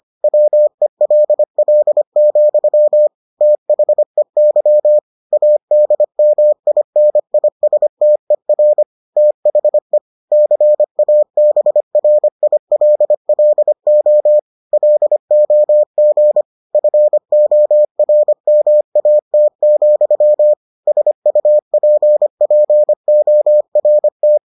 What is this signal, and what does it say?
Morse code, heard as a keyed tone, about 25 wpm, 600 Hz. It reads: WELL, THEY ADMINISTER THE CABRILLO LOG FORMAT, SUPPORT